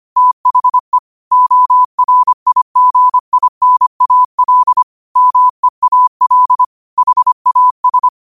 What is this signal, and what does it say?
Morse code, heard as a keyed tone, about 25 wpm, 1000 Hz. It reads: THE ORIGINAL MEAL HAS